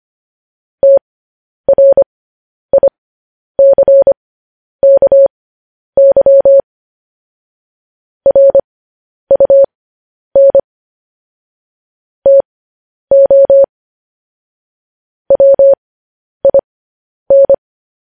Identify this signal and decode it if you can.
Morse code, heard as a keyed tone, about 25 wpm, spaced as for 10 wpm, 550 Hz. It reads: TRICKY RUN TO WIN
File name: podcast / News headlines-1x-25x10